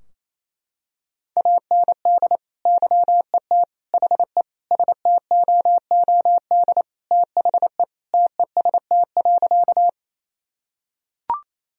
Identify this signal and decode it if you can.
Morse code, heard as a keyed tone, about 27 wpm, 700 Hz. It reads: AND YET HE STOOD THE TEST.
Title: Sentences from Top 1000 Words - Review Part 3 - 28wpm